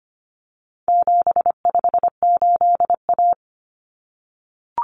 Morse code, heard as a keyed tone, about 25 wpm, 700 Hz. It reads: 758A
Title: Common Cut Numbers - 4-Digit 25wpm